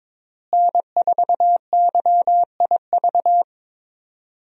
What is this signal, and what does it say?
Morse code, heard as a keyed tone, about 22 wpm, 700 Hz. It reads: N4YIV
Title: USA Call Signs - 5 Characters - Repeated 2x - 22wpm